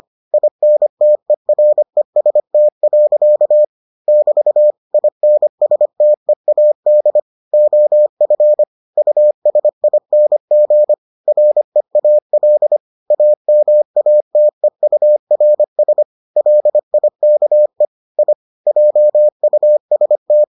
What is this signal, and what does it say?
Morse code, heard as a keyed tone, about 25 wpm, 600 Hz. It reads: INTEREST. <BT> INSTEAD OF USING REAL AMATEURS LIKE I JUST